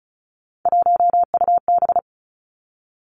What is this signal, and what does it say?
Morse code, heard as a keyed tone, about 36 wpm, 700 Hz. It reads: 1UB
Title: Three Challenging Letter-Number Combinations 35wpm